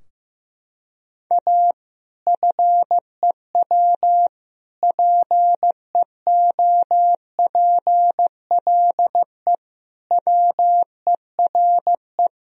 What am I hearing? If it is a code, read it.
Morse code, heard as a keyed tone, about 15 wpm, 700 Hz. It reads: A FEW PEOPLE WERE
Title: Sentences from Top 1000 Words - Review Part 3 - 15wpm